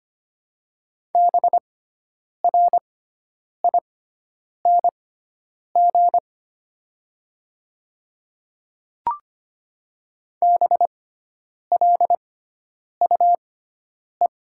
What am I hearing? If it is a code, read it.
Morse code, heard as a keyed tone, about 25 wpm, spaced as for 9 wpm, 700 Hz. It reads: BRING BLUE
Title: 6x Character Spacing - Top 500 Words - 25wpm